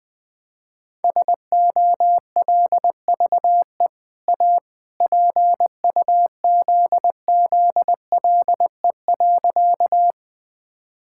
Morse code, heard as a keyed tone, about 20 wpm, 700 Hz. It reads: SOLVE A PUZZLE.
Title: Easy Sentences with Hints - 20wpm